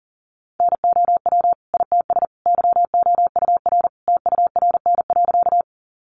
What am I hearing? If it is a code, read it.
Morse code, heard as a keyed tone, about 40 wpm, 700 Hz. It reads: NOW ITS YOUR TURN.
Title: Sentences from Top 600 Words 40wpm